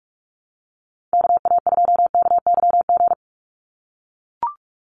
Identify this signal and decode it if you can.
Morse code, heard as a keyed tone, about 44 wpm, 700 Hz. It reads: KA2KYG
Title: USA Call Signs 45wpm